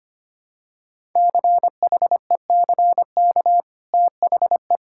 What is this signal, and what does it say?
Morse code, heard as a keyed tone, about 25 wpm, 700 Hz. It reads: CHECK THE
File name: Easy Sentences with Hints - 25wpm